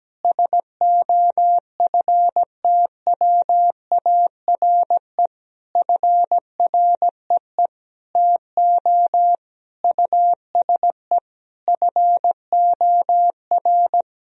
Morse code, heard as a keyed tone, about 17 wpm, 700 Hz. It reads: SOFTWARE FREE TO USE FOR
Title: Sentences from Top 1000 Words - Review Part 3 - 17wpm